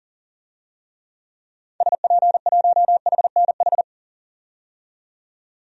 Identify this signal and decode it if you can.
Morse code, heard as a keyed tone, about 38 wpm, 700 Hz. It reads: SP1HNH
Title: DXCC-100 Call Signs - 6 Characters - 40wpm